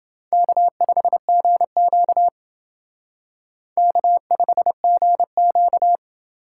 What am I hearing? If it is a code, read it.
Morse code, heard as a keyed tone, about 28 wpm, 700 Hz. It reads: K5GQ K5GQ